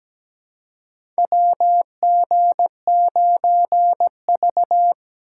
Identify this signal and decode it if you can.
Morse code, heard as a keyed tone, about 17 wpm, 700 Hz. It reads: WG9V
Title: USA Call Signs - 4 Characters - 2x Character Spacing - 17wpm